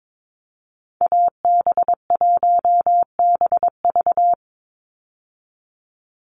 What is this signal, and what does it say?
Morse code, heard as a keyed tone, about 22 wpm, 700 Hz. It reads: AB1BV
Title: USA Call Signs 22wpm